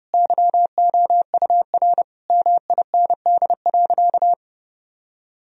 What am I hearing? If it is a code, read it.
Morse code, heard as a keyed tone, about 30 wpm, 700 Hz. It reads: YOUR MIND.
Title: Sentences from Top 1000 Words - Review Part 3 - 30wpm